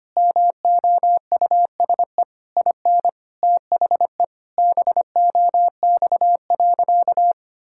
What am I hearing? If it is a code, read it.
Morse code, heard as a keyed tone, about 25 wpm, 700 Hz. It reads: MOUSE IN THE BOX.
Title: Sentences from Top 1000 Words - Review Part 5 - 25wpm